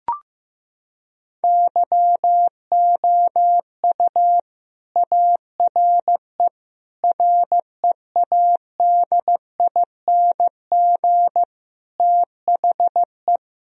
Morse code, heard as a keyed tone, about 16 wpm, 700 Hz. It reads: YOU ARE READING THE
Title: Sentences from Top 1000 Words - Review Part 3 - 15wpm